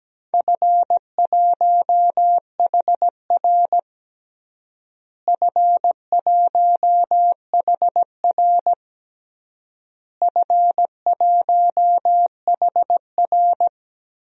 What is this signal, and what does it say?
Morse code, heard as a keyed tone, about 17 wpm, 700 Hz. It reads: F1HR F1HR F1HR
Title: DXCC-100 Call Signs - 4 Characters - Repeated 3x - 17wpm